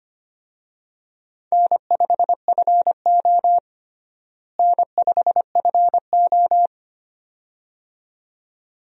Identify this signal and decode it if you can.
Morse code, heard as a keyed tone, about 25 wpm, 700 Hz. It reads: N5FO N5FO
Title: USA Call Signs - 4 Characters - Repeated 2x - 25wpm